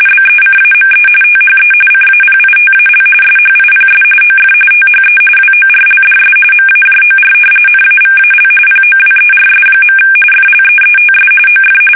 RTTY, about 45.45 baud. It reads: BECAME THE CAPITAL SHIPS OF THE WORLDS MAJOR NAVAL POWERS. FLEETS OF
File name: fsk850 / battleshipnj